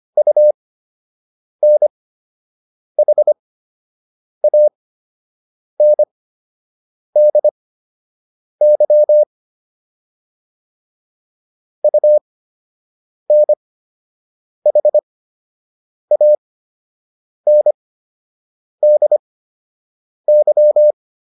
Morse code, heard as a keyed tone, about 25 wpm, spaced as for 7 wpm, 600 Hz. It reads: UNHANDY UNHANDY